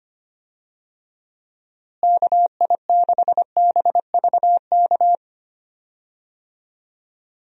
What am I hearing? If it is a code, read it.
Morse code, heard as a keyed tone, about 25 wpm, 700 Hz. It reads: KI6BVK
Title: USA Call Signs - 6 Characters - Repeated 3x - 25wpm